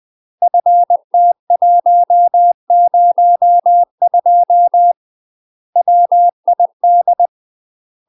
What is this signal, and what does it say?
Morse code, heard as a keyed tone, about 20 wpm, 700 Hz. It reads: FT102 WID
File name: UK MASTER QSOs TRAINING / 20WPM / UK MASTER QSOs 0002